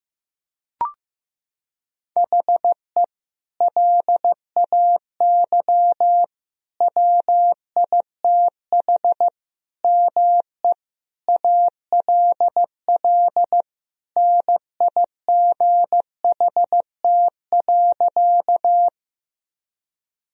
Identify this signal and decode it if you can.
Morse code, heard as a keyed tone, about 15 wpm, 700 Hz. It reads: HE LAY WITH ME ALL NIGHT.
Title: Sentences from Top 1000 Words - Review Part 2 - 15wpm